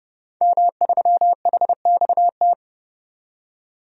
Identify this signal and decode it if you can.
Morse code, heard as a keyed tone, about 30 wpm, 700 Hz. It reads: M3HXT
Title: DXCC-100 Call Signs - 5 Characters - 30wpm